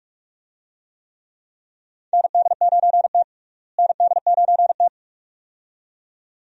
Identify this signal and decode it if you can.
Morse code, heard as a keyed tone, about 46 wpm, 700 Hz. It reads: ND9T ND9T
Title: USA Call Signs - 4 Characters - Repeated 2x - 45wpm